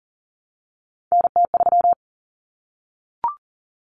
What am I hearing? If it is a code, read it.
Morse code, heard as a keyed tone, about 40 wpm, 700 Hz. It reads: NT3
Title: Common Cut Numbers - 3-Digit 40wpm